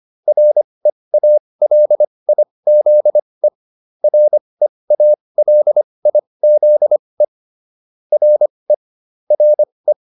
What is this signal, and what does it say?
Morse code, heard as a keyed tone, about 25 wpm, spaced as for 20 wpm, 600 Hz. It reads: REALIZE REALIZE RE RE
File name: RE7 20